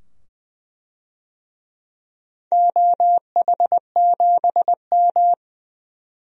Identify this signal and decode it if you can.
Morse code, heard as a keyed tone, about 21 wpm, 700 Hz. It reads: OH7M
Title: DXCC-100 Call Signs - 4 Characters - 20wpm